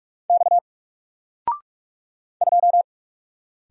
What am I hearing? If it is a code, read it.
Morse code, heard as a keyed tone, about 44 wpm, 700 Hz. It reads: X 2
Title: Single Letter-Number 45wpm